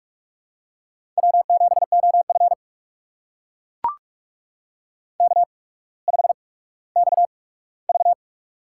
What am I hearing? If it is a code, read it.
Morse code, heard as a keyed tone, about 44 wpm, spaced as for 19 wpm, 700 Hz. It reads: W7OF K5XV
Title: USA Call Signs - 4 Characters - 8x Character Spacing - 45wpm